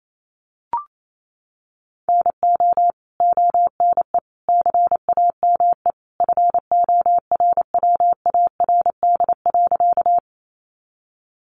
Morse code, heard as a keyed tone, about 28 wpm, 700 Hz. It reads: NO ONE CAME FORWARD.
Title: Sentences from Top 1300 Words 28wpm